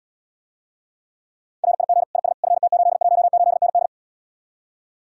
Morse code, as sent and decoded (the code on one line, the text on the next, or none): .-. .. --. / .. ... / ..-. - ---.. .---- ---.. -. -..
RIG IS FT818ND